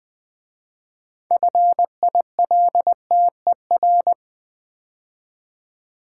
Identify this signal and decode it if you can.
Morse code, heard as a keyed tone, about 20 wpm, 700 Hz. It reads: FILTER
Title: 6 Letter Words - First 3 Letters Given - 20wpm